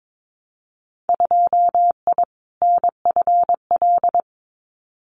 Morse code, heard as a keyed tone, about 22 wpm, 700 Hz. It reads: 2I NFL
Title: Winter Field Day Exchanges 22wpm